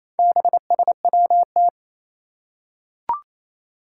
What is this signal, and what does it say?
Morse code, heard as a keyed tone, about 27 wpm, 700 Hz. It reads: BSWT
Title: Four Challenging Letter-Number Combinations 28wpm